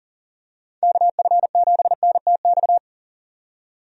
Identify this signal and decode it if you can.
Morse code, heard as a keyed tone, about 40 wpm, 700 Hz. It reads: KF7NTX
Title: USA Call Signs - 6 Characters - 40wpm